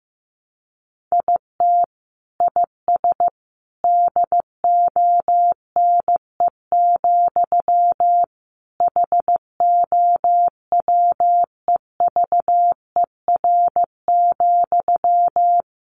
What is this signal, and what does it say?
Morse code, heard as a keyed tone, about 15 wpm, 700 Hz. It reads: IT IS DONE, HOWEVER,
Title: Sentences from Top 600 Words 15wpm